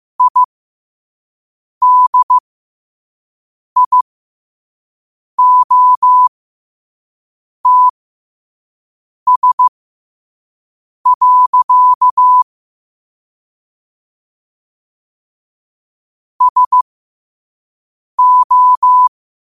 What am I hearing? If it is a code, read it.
Morse code, heard as a keyed tone, about 15 wpm, spaced as for 5 wpm, 1000 Hz. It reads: IDIOTS. SO
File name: qotdfiles / 5wpm